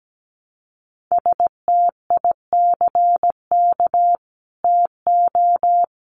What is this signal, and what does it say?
Morse code, heard as a keyed tone, about 17 wpm, 700 Hz. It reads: STICK TO
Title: Sentences from Top 1200 Words 17wpm